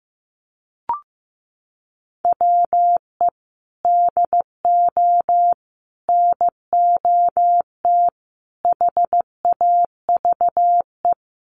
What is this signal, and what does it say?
Morse code, heard as a keyed tone, about 15 wpm, 700 Hz. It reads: WE DO NOT HAVE
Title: Sets of 5 Words 15wpm